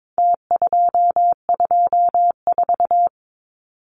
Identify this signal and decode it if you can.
Morse code, heard as a keyed tone, about 22 wpm, 700 Hz. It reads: T224
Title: Numbers - Contest Style - 4-Digit 22wpm